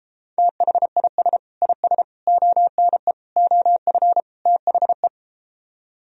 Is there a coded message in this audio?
Yes: Morse code from 0.4 to 5.1 s, heard as a keyed tone, about 34 wpm, 700 Hz. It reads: THIS IS ONE OF THE